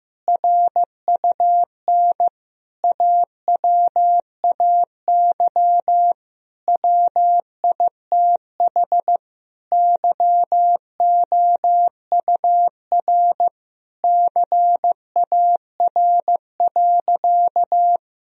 Morse code, heard as a keyed tone, about 16 wpm, 700 Hz. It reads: RUN AWAY WITH YOUR CAR.
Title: Sentences from Top 1000 Words - Review Part 1 - 15wpm